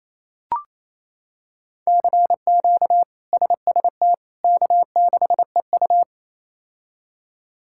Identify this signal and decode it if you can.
Morse code, heard as a keyed tone, about 28 wpm, 700 Hz. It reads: CQ SST K6EU
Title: K1USN SST QSO Practice - 28wpm